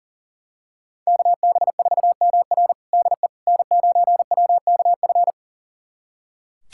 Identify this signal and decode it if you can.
Morse code, heard as a keyed tone, about 40 wpm, 700 Hz. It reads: KB4MR DE N9WKF
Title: USA Callsign DE Callsign - 5 Characters - 40wpm